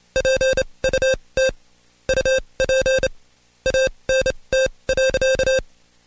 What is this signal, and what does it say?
Morse code, heard as a keyed tone, about 27 wpm, spaced as for 22 wpm, 530 Hz. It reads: PUT UP ANT.
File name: Code Quick / AudioDownld2 / Seg137-Stepper10 21WPM